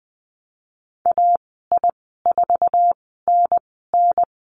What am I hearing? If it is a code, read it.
Morse code, heard as a keyed tone, about 20 wpm, spaced as for 15 wpm, 700 Hz. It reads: AI4NN
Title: 055 Lesson 10g - Letter r - Call Signs 20wpm